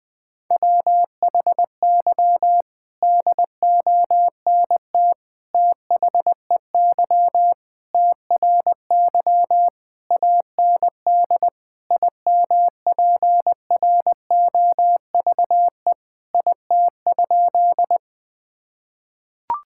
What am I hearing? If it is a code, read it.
Morse code, heard as a keyed tone, about 21 wpm, 700 Hz. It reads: WHY DONT THEY TRY AND IMPROVE IT?
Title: Sentences from Top 1000 Words - Review Part 5 - 20wpm